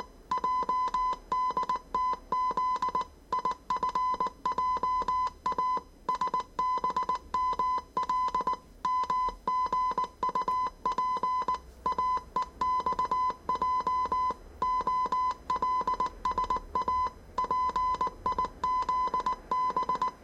Morse code, heard as a keyed tone, about 19 wpm, 1010 Hz. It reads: JDTZ IFJA SBML MGUP AEXJ OLSA PIZB
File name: 业余无线电 / ●未分类 / 字码80